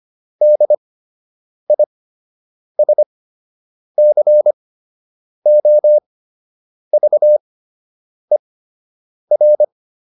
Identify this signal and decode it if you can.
Morse code, heard as a keyed tone, about 25 wpm, spaced as for 8 wpm, 600 Hz. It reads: DISCOVER